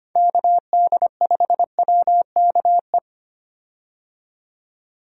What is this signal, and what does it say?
Morse code, heard as a keyed tone, about 25 wpm, 700 Hz. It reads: KD5WKE
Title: USA Call Signs - 6 Characters - 25wpm